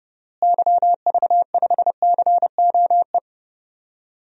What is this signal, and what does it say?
Morse code, heard as a keyed tone, about 30 wpm, 700 Hz. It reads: YV5COE
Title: DXCC-100 Call Signs - 6 Characters - Repeated 3x - 30wpm